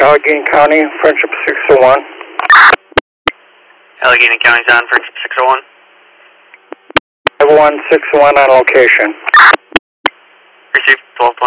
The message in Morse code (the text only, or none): none